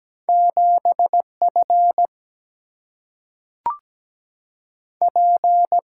Morse code, heard as a keyed tone, about 17 wpm, 700 Hz. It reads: 7 F P
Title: Single-Letter Single-Number 17wpm